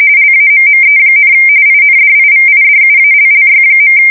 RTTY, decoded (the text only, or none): LAMME, INDIVIDUALLY.